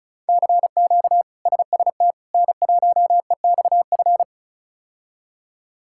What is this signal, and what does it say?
Morse code, heard as a keyed tone, about 36 wpm, 700 Hz. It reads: CQ SST N1EXF
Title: K1USN SST QSO Practice - 35wpm